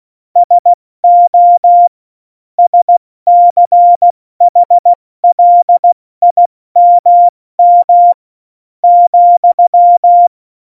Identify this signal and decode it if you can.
Morse code, heard as a keyed tone, about 16 wpm, spaced as for 13 wpm, 700 Hz. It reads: SO SCHLIMM,
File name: Morsekurs / Witze / Witz 26. Dezember 2017, 70 F80